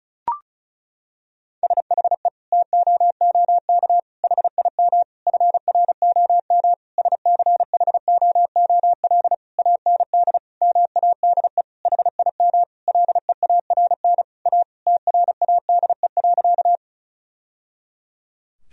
Morse code, heard as a keyed tone, about 36 wpm, 700 Hz. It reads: SHE TOOK HIM FROM SCHOOL AND MADE HIM LEARN A TRADE.
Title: Sentences from Top 1000 Words - Review Part 4 - 35wpm